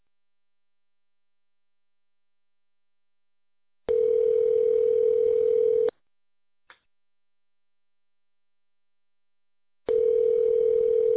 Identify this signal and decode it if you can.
Ringback tone (North America), 440 + 480 Hz.